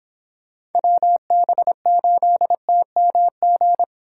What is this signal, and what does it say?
Morse code, heard as a keyed tone, about 28 wpm, 700 Hz. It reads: WB8TMG